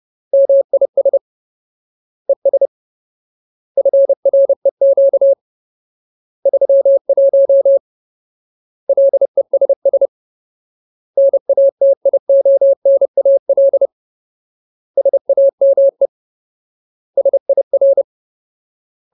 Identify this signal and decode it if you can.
Morse code, heard as a keyed tone, about 30 wpm, 550 Hz. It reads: MIS ES FREQ 31 LESS NATIONAL SAME SIR
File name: Ultimate Word List / 30WPM Ultimate